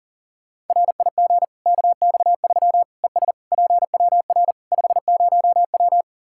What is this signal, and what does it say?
Morse code, heard as a keyed tone, about 40 wpm, 700 Hz. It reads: RIG KX3 ES PWR 50W
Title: Practice QSO - Exchange 2 - Info - 40wpm